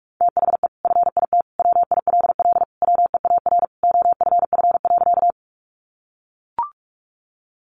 Morse code, heard as a keyed tone, about 44 wpm, 700 Hz. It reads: THE FIT WILL WEAR OFF.
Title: Sentences from Top 1500 Words 45wpm